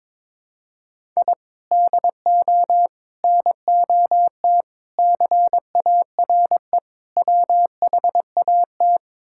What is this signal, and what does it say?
Morse code, heard as a keyed tone, about 21 wpm, 700 Hz. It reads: I DO NOT CARE WHAT
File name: practice / 22wpm / Build A Phrase - 5-Word Phrases - 22wpm